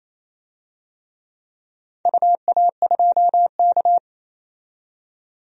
Morse code, heard as a keyed tone, about 28 wpm, 700 Hz. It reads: UA2K
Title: DXCC-100 Call Signs - 4 Characters - 28wpm